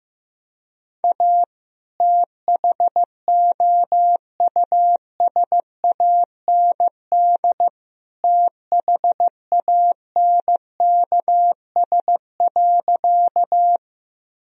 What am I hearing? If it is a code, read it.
Morse code, heard as a keyed tone, about 16 wpm, 700 Hz. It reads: A THOUSAND THANKS.
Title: Sentences from Top 1000 Words - Review Part 3 - 15wpm